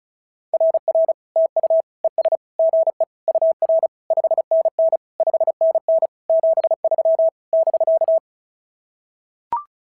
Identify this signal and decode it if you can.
Morse code, heard as a keyed tone, about 36 wpm, 650 Hz. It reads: RR TU ES GE UR 5NN 5NN 73 <BK>
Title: SOTA QSO Practice 35wpm